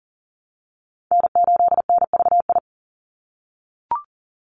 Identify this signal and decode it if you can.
Morse code, heard as a keyed tone, about 40 wpm, 700 Hz. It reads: N8NVI
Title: USA Call Signs 40wpm